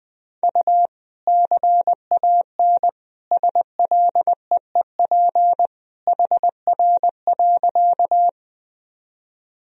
Morse code, heard as a keyed tone, about 20 wpm, 700 Hz. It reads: U CAN SLEEP HR.
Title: Sentences from Top 500 Words and Abbreviations 20wpm